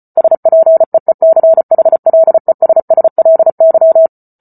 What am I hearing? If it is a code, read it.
Morse code, heard as a keyed tone, about 34 wpm, 650 Hz. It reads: SPEECHLESSLY